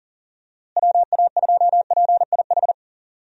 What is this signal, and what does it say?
Morse code, heard as a keyed tone, about 40 wpm, 700 Hz. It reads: WA2PIH
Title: USA Call Signs - 6 Characters - Repeated 3x - 40wpm